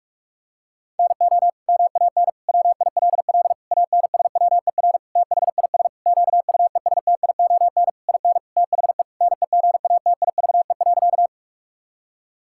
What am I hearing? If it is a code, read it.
Morse code, heard as a keyed tone, about 46 wpm, 700 Hz. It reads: NO MAN WILL ANSWER THIS QUESTION IN THE NEGATIVE.
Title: Sentences from Top 1000 Words - Review Part 6 - 45wpm